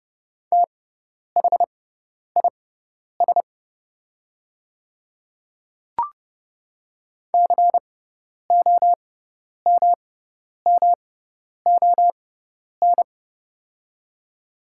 Morse code, heard as a keyed tone, about 30 wpm, spaced as for 10 wpm, 700 Hz. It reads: THIS COMMON
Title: 6x Character Spacing - Top 500 Words - 30wpm